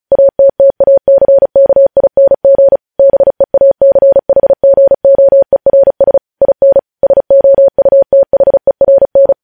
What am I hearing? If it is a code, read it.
Morse code, heard as a keyed tone, about 36 wpm, 550 Hz. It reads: ATTACKING BEACHGOERS IN SOUTHERN
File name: podcast / News headlines-1x-35x35